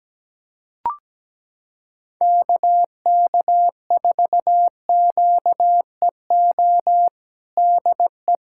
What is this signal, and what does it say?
Morse code, heard as a keyed tone, about 17 wpm, 700 Hz. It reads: KK4QEO DE
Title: USA Callsign DE Callsign 17wpm